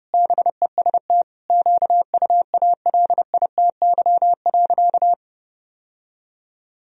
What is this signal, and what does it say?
Morse code, heard as a keyed tone, about 30 wpm, 700 Hz. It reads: BEST QUALITY.
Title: Sentences from Top 1000 Words - Review Part 4 - 30wpm